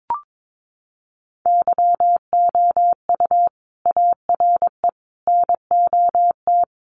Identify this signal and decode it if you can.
Morse code, heard as a keyed tone, about 22 wpm, 700 Hz. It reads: YOU ARE NOT
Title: Sentences from Top 1200 Words 22wpm